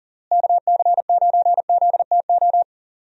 Morse code, heard as a keyed tone, about 40 wpm, 700 Hz. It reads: KC9ZTO
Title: USA Call Signs - 6 Characters - Repeated 3x - 40wpm